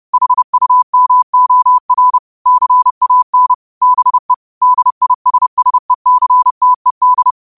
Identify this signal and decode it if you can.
Morse code, heard as a keyed tone, about 30 wpm, 1000 Hz. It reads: HUMOR CAN BE DISSECTED
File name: qotdfiles / 30wpm